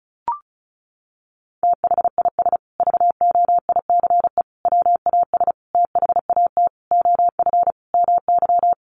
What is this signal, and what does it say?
Morse code, heard as a keyed tone, about 34 wpm, 700 Hz. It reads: THIS VOICE WAS THAT OF MY
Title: Sentences from Top 500 Words 35wpm